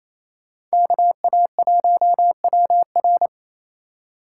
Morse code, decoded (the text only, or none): KA1WR